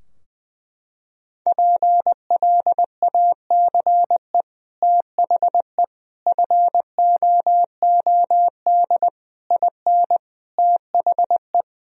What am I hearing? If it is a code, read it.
Morse code, heard as a keyed tone, about 21 wpm, 700 Hz. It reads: PLACE THE FOOD IN THE
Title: Sentences from Top 1000 Words - Review Part 4 - 20wpm